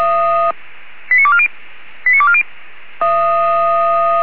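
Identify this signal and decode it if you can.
Five-tone selective call (ZVEI), heard as quick runs of five tones, tones of about 70 ms. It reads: 89240 89240
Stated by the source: Received on an SDR.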